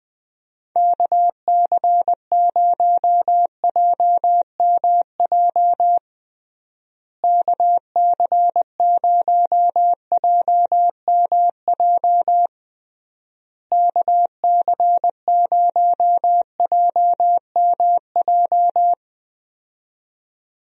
Morse code, heard as a keyed tone, about 20 wpm, 700 Hz. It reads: KC0JMJ KC0JMJ KC0JMJ
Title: USA Call Signs - 6 Characters - Repeated 3x - 20wpm